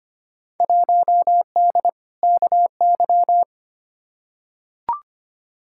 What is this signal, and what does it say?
Morse code, heard as a keyed tone, about 25 wpm, 700 Hz. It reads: 1D KY
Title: Field Day Exchanges - 25wpm